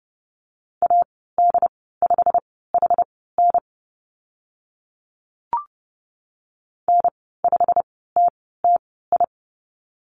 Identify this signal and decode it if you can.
Morse code, heard as a keyed tone, about 30 wpm, spaced as for 17 wpm, 690 Hz. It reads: AD5HN N5TTI
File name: Lesson 15g - Number 5 - Call Signs 30wpm